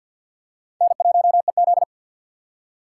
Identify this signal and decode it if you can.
Morse code, heard as a keyed tone, about 53 wpm, 700 Hz. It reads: N1EZ
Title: USA Call Signs - 2x Character Spacing - 50wpm